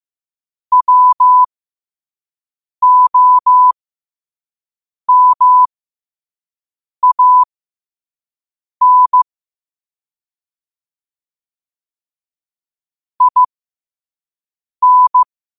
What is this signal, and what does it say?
Morse code, heard as a keyed tone, about 15 wpm, spaced as for 5 wpm, 1000 Hz. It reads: WOMAN IN
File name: qotdfiles / 5wpm